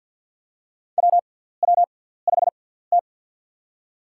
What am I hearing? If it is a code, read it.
Morse code, heard as a keyed tone, about 53 wpm, spaced as for 17 wpm, 700 Hz. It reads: WW5T
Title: USA Call Signs - 6x Character Spacing - 50wpm